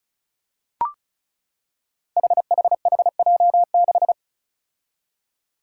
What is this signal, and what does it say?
Morse code, heard as a keyed tone, about 36 wpm, 700 Hz. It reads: HHHJ6
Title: Five Challenging Letter-Number Combinations 35wpm